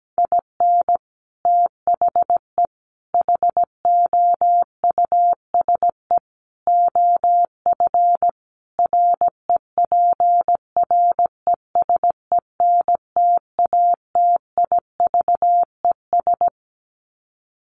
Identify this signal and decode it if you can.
Morse code, heard as a keyed tone, about 17 wpm, 700 Hz. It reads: IN THE HOUSE OF REPRESENTATIVES
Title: Sets of 5 Words Encore 17wpm